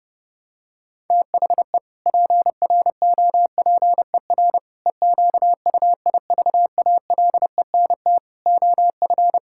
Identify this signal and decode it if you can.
Morse code, heard as a keyed tone, about 30 wpm, 700 Hz. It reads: THE PROPER EQUIVALENT OF
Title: Sentences from Top 1600 Words 30wpm